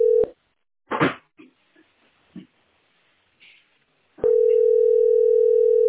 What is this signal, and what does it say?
Ringback tone (North America), 440 + 480 Hz.